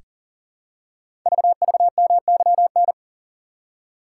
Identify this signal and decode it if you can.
Morse code, heard as a keyed tone, about 41 wpm, 700 Hz. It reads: VVMYN
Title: Code Groups - 5 Letters - 40wpm